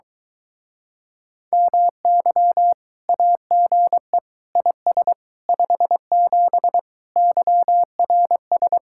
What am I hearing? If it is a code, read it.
Morse code, heard as a keyed tone, about 23 wpm, 700 Hz. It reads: MY AGE IS 57 YRS